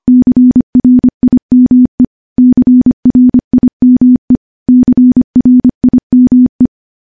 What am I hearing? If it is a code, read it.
Morse code, heard as a keyed tone, about 25 wpm, 250 Hz. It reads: CRIME CRIME CRIME
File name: morse / morse-training-25WPM